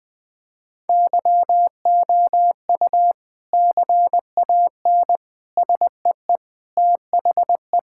Morse code, heard as a keyed tone, about 20 wpm, 700 Hz. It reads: YOU CAN SEE THE
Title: Sentences with 3-Word Phrases - 20wpm